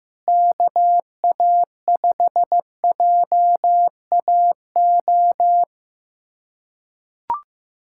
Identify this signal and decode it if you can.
Morse code, heard as a keyed tone, about 15 wpm, 700 Hz. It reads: KA5JAO
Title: USA Callsign DE Callsign - 6 Characters - 15wpm